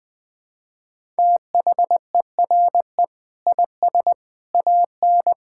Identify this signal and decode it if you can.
Morse code, heard as a keyed tone, about 20 wpm, 700 Hz. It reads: THERE IS AN
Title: Sentences with 8-Word Phrases - Spoken Beginning - 20wpm